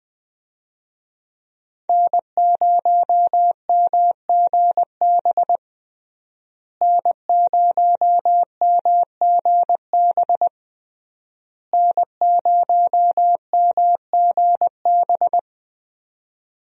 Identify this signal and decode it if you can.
Morse code, heard as a keyed tone, about 21 wpm, 700 Hz. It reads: N0MGB N0MGB N0MGB
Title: USA Call Signs - 5 Characters - Repeated 3x - 20wpm